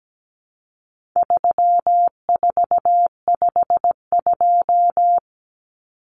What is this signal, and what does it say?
Morse code, heard as a keyed tone, about 17 wpm, 700 Hz. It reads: 3452
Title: Numbers - 4-Digit 17wpm